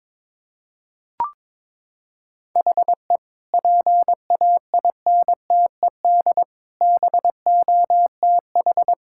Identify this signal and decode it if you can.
Morse code, heard as a keyed tone, about 22 wpm, 700 Hz. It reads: HE PAINTED BOTH
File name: practice / 22wpm / Sentences with 4-Word Phrases - 22wpm